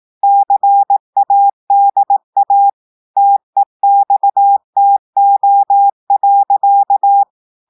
Morse code, heard as a keyed tone, about 18 wpm, 800 Hz. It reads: CADA TEXTO.
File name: audios / portugais / portugais texte en clair 1 vitesse 18 mots minutes